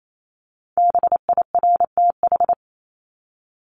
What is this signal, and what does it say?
Morse code, heard as a keyed tone, about 28 wpm, 700 Hz. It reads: BIRTH
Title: Top 751-1000 Words 28wpm